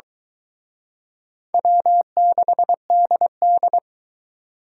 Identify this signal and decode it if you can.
Morse code, heard as a keyed tone, about 23 wpm, 700 Hz. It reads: W6DD